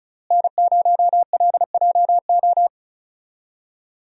Morse code, heard as a keyed tone, about 36 wpm, 700 Hz. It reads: N0LJO
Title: USA Call Signs - 5 Characters - Repeated 3x - 35wpm